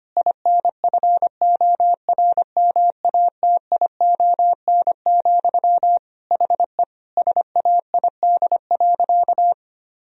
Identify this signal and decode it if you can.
Morse code, heard as a keyed tone, about 25 wpm, 700 Hz. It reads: INFORMATION, HE SAID.
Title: Sentences from Top 1000 Words - Review Part 3 - 25wpm